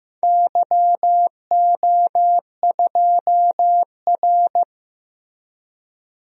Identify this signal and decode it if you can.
Morse code, heard as a keyed tone, about 15 wpm, 700 Hz. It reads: YO2R
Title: DXCC-100 Call Signs - 4 Characters - Repeated 3x - 15wpm